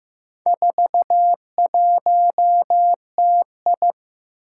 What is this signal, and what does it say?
Morse code, heard as a keyed tone, about 15 wpm, 700 Hz. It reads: 41TI
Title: Four Challenging Letter-Number Combinations 15wpm